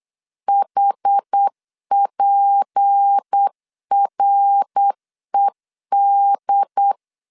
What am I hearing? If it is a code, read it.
Morse code, heard as a keyed tone, about 8 wpm, 800 Hz. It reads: HPRED
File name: LEZIONI / 6 PKX0 / 6 PKX 8wpm 3041